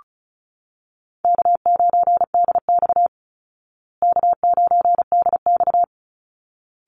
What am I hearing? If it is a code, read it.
Morse code, heard as a keyed tone, about 34 wpm, 700 Hz. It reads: K9DX K9DX
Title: Call Signs Repeated 3x - 35wpm